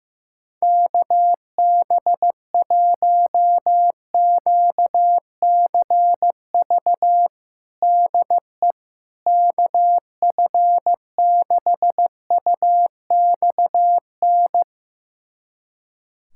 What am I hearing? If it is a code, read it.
Morse code, heard as a keyed tone, about 15 wpm, 700 Hz. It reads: KB1QCV DE KF6UXN
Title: USA Callsign DE Callsign - 6 Characters - 15wpm